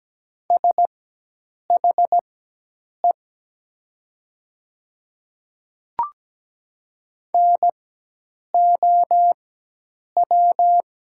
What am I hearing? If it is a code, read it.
Morse code, heard as a keyed tone, about 17 wpm, spaced as for 8 wpm, 700 Hz. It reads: SHE NOW
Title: 4x Character Spacing - Top 100 Words - 17wpm